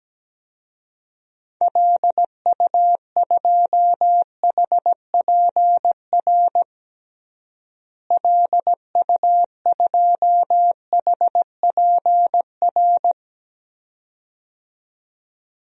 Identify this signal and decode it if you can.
Morse code, heard as a keyed tone, about 17 wpm, 700 Hz. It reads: LU2HPR LU2HPR
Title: DXCC-100 Call Signs - 6 Characters - Repeated 2x - 17wpm